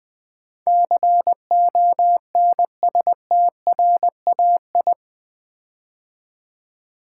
Morse code, heard as a keyed tone, about 20 wpm, 700 Hz. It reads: CONSTRAI
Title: Build A Word - 4 Letters At A Time - 10-Letter Words - 20wpm